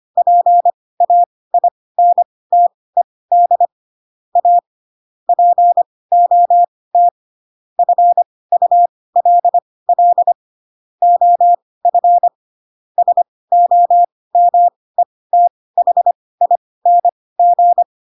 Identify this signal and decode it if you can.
Morse code, heard as a keyed tone, about 25 wpm, spaced as for 18 wpm, 700 Hz. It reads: PAINTED A POT FULL OF SOMETHING
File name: Pinocchio ch3 12 25 ik1hge